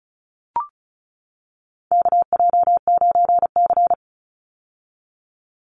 Morse code, heard as a keyed tone, about 34 wpm, 700 Hz. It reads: KJ9C
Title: Call Signs - 35wpm